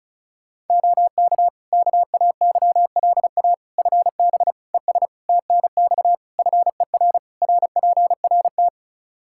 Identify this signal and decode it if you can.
Morse code, heard as a keyed tone, about 34 wpm, 700 Hz. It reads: OK KAYLA FB ES TNX FER RPRT
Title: K1USN SST QSO Practice 35wpm